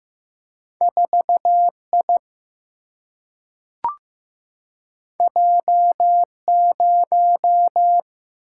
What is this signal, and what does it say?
Morse code, heard as a keyed tone, about 15 wpm, 700 Hz. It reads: 4 I J 0
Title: Single-Letter Single-Number 15wpm